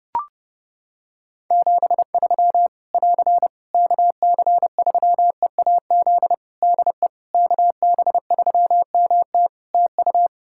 Morse code, heard as a keyed tone, about 28 wpm, 700 Hz. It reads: 73 <AR> KC3EAZ DE KB3MT TU
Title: Practice QSO - Exchange 4 - Ending - 30wpm